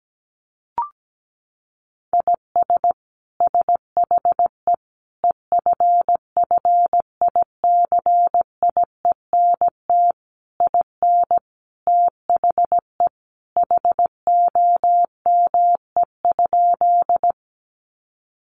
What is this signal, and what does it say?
Morse code, heard as a keyed tone, about 17 wpm, 700 Hz. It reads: IS SHE EFFICIENT IN THE HOME?
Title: Sentences from Top 1300 Words 17wpm